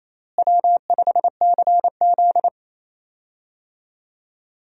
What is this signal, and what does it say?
Morse code, heard as a keyed tone, about 28 wpm, 700 Hz. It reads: W5CZ
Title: USA Call Signs - 4 Characters - Repeated 2x - 28wpm